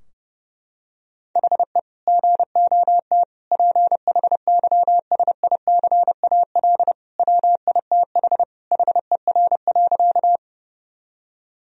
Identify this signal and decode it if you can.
Morse code, heard as a keyed tone, about 32 wpm, 700 Hz. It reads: HE GOT PHYSICAL WITH HER.
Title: Sentences from Top 1000 Words - Review Part 4 - 30wpm